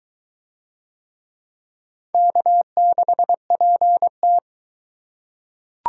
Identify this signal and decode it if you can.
Morse code, heard as a keyed tone, about 22 wpm, 700 Hz. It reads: K6PT